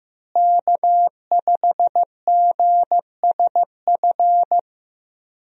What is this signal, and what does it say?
Morse code, heard as a keyed tone, about 15 wpm, 700 Hz. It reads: K5GSF
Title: USA Call Signs - 5 Characters - Repeated 3x - 15wpm